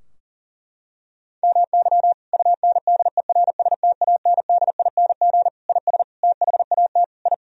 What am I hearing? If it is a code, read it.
Morse code, heard as a keyed tone, about 40 wpm, 700 Hz. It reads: MY UNDERSTANDING IS THAT I
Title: Sentences from Top 1000 Words - Review Part 3 - 40wpm